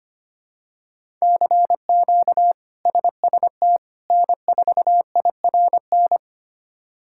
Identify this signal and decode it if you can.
Morse code, heard as a keyed tone, about 25 wpm, 700 Hz. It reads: CQ SST N4IRN
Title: K1USN SST QSO Practice - 25wpm